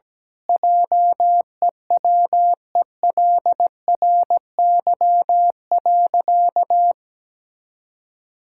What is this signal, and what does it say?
Morse code, heard as a keyed tone, about 17 wpm, 700 Hz. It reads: JEWELRY.
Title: Pangram Challenge v2 - 17wpm